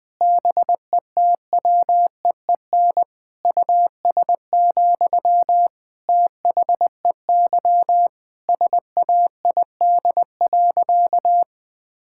Morse code, heard as a keyed tone, about 20 wpm, 700 Hz. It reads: BETWEEN US, THEY SAID.
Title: Sentences from Top 1600 Words 20wpm